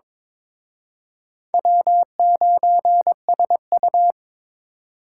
Morse code, heard as a keyed tone, about 26 wpm, 700 Hz. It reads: W9SU
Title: Speed Racing - USA Call Signs 22wpm